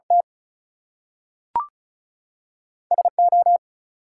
Morse code, heard as a keyed tone, about 36 wpm, 700 Hz. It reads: T SO
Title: Two Letter Words - 35wpm